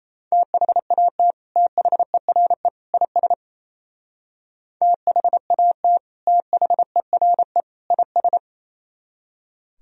Morse code, heard as a keyed tone, about 35 wpm, 700 Hz. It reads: THAT THERE IS THAT THERE IS
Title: Speed-Racing - Sets of 3 Words 28wpm